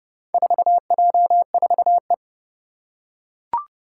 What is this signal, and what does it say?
Morse code, heard as a keyed tone, about 31 wpm, 700 Hz. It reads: 4J4E
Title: Four Challenging Letter-Number Combinations 30wpm